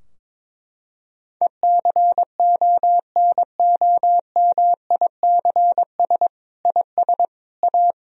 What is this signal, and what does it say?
Morse code, heard as a keyed tone, about 22 wpm, 700 Hz. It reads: ECONOMICS IS A
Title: Sentences from Top 1000 Words - Review Part 3 - 22wpm